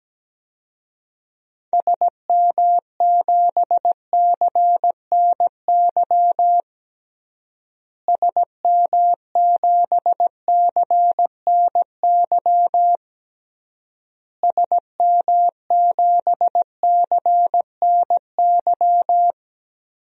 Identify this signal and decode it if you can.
Morse code, heard as a keyed tone, about 17 wpm, 700 Hz. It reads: SM7CNY SM7CNY SM7CNY
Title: DXCC-100 Call Signs - 6 Characters - Repeated 3x - 17wpm